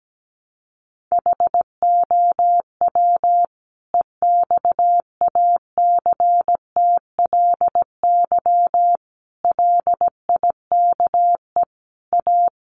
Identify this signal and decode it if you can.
Morse code, heard as a keyed tone, about 17 wpm, 700 Hz. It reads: HOW EXACTLY LIKE A
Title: Sentences from Top 900 Words 17wpm